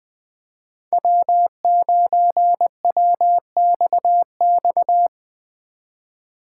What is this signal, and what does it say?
Morse code, heard as a keyed tone, about 21 wpm, 700 Hz. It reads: W9WXX
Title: Speed Racing - USA 1x3 Call Signs 20wpm